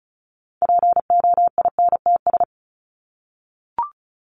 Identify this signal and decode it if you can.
Morse code, heard as a keyed tone, about 34 wpm, 700 Hz. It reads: POINTS
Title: ES:S Suffix - Top 100 Words with Base Words 35wpm